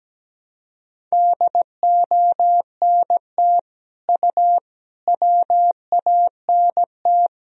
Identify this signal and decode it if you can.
Morse code, heard as a keyed tone, about 17 wpm, 700 Hz. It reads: DONT U WANT
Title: Sentences from Top 300 Words and Abbreviations 17wpm